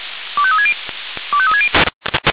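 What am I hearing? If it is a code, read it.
Five-tone selective call (ZVEI), heard as quick runs of five tones, tones of about 70 ms. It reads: 25360 25360
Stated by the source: ZVEI five-tone selective call received on an SDR.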